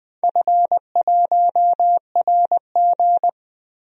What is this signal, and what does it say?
Morse code, heard as a keyed tone, about 20 wpm, 700 Hz. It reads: F1RG
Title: DXCC-100 Call Signs - 4 Characters - 20wpm